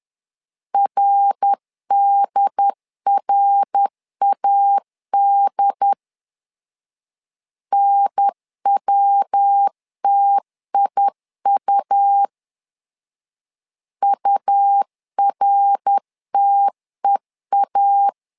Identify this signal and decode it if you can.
Morse code, heard as a keyed tone, about 10 wpm, 800 Hz. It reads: RDRAD NWTIU URTEA
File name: LEZIONI / 3 MWRO / 3 MWR 10wpm 1952